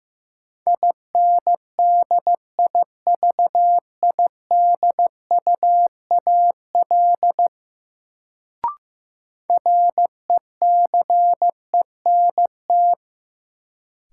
Morse code, heard as a keyed tone, about 15 wpm, 700 Hz. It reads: INDIVIDUAL RECENT
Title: Sentences from Top 1400 Words - New Word Drill Alt - 15wpm